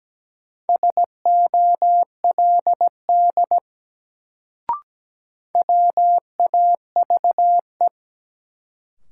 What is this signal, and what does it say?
Morse code, heard as a keyed tone, about 17 wpm, 700 Hz. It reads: SOLD WAVE
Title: Sentences from Top 1400 Words - New Word Drill Alt - 17wpm